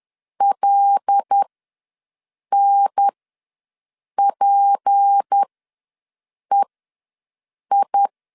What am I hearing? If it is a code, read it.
Morse code, heard as a keyed tone, about 10 wpm, spaced as for 6 wpm, 800 Hz. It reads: LNPEI
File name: LEZIONI / 7 GF9Z / 7 GF ws 10wpm 3918